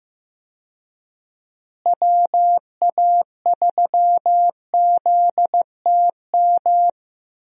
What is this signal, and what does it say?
Morse code, heard as a keyed tone, about 15 wpm, 700 Hz. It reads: WA3ZTM
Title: USA Call Signs - 6 Characters - Repeated 2x - 15wpm